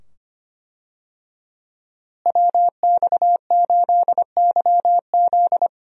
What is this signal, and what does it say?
Morse code, heard as a keyed tone, about 25 wpm, 700 Hz. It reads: WX8YZ
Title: USA Call Signs - 5 Characters - Repeated 2x - 25wpm